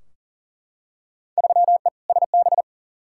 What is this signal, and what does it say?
Morse code, heard as a keyed tone, about 40 wpm, 700 Hz. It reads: 3E SB
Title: Field Day Exchanges Encore - 40wpm